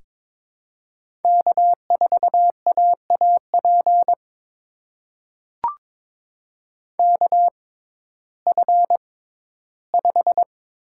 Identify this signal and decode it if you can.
Morse code, heard as a keyed tone, about 22 wpm, spaced as for 13 wpm, 700 Hz. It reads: K4AAP KF5
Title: USA Call Signs - 5 Characters - 6x Character Spacing - 22wpm